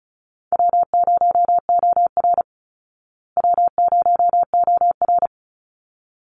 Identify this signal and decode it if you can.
Morse code, heard as a keyed tone, about 34 wpm, 700 Hz. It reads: W0OR W0OR
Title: Call Signs Repeated 3x - 35wpm